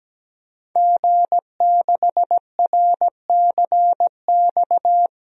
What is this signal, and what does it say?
Morse code, heard as a keyed tone, about 17 wpm, 700 Hz. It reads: G6RCX
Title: DXCC-100 Call Signs - 5 Characters - Repeated 3x - 17wpm